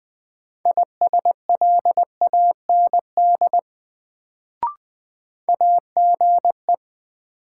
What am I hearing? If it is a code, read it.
Morse code, heard as a keyed tone, about 20 wpm, 700 Hz. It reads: ISLAND AGE
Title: Sentences from Top 500 Words - New Word Drill - 20wpm